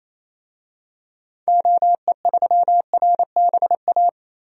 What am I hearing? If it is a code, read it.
Morse code, heard as a keyed tone, about 28 wpm, 700 Hz. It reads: OE3RBA
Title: DXCC-100 Call Signs - 6 Characters - Repeated 3x - 28wpm